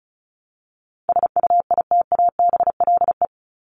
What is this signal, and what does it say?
Morse code, heard as a keyed tone, about 34 wpm, 700 Hz. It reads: SUITABLE
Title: Top 1251-1500 Words 35wpm